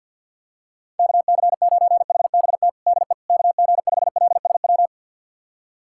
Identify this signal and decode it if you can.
Morse code, heard as a keyed tone, about 47 wpm, 700 Hz. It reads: KC9HBT DE KG5LSW
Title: USA Callsign DE Callsign 50wpm